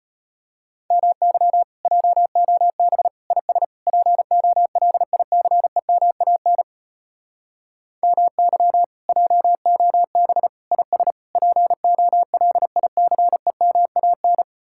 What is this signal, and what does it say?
Morse code, heard as a keyed tone, about 33 wpm, 700 Hz. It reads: MY JOB IS POLICEMAN MY JOB IS POLICEMAN